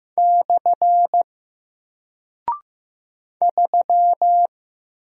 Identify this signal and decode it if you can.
Morse code, heard as a keyed tone, about 15 wpm, 700 Hz. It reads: / 3
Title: Single Character (Letter:Number:Punctuation:Prosign) Alt 15wpm